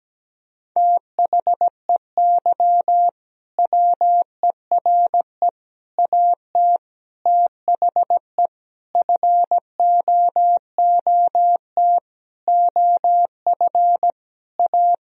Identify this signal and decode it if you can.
Morse code, heard as a keyed tone, about 17 wpm, 700 Hz. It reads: THEY WERE AT THE FOOT OF A
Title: Sentences from Top 1000 Words - Review Part 3 - 17wpm